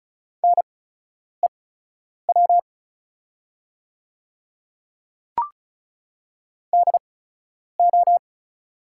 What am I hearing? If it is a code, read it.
Morse code, heard as a keyed tone, about 36 wpm, spaced as for 10 wpm, 690 Hz. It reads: NEW DO
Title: 8x Character Spacing - Top 200 Words - 35wpm